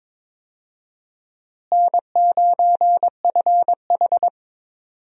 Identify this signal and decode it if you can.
Morse code, heard as a keyed tone, about 22 wpm, 700 Hz. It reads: N9FH
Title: USA Call Signs - 4 Characters - Repeated 3x - 22wpm